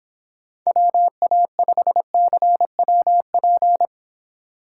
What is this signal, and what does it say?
Morse code, heard as a keyed tone, about 25 wpm, 700 Hz. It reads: WA5CWP